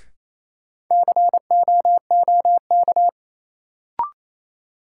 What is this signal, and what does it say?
Morse code, heard as a keyed tone, about 28 wpm, 700 Hz. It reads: COOK
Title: Sentences from Top 700 Words - New Word Drill - 28wpm